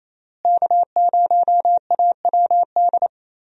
Morse code, heard as a keyed tone, about 28 wpm, 700 Hz. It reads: K0AWD
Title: USA Call Signs - 5 Characters - Repeated 2x - 28wpm